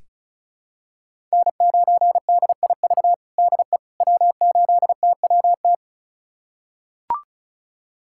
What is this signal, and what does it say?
Morse code, heard as a keyed tone, about 34 wpm, 700 Hz. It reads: N9DIV DE W8TWT
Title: USA Callsign DE Callsign - 5 Characters - 35wpm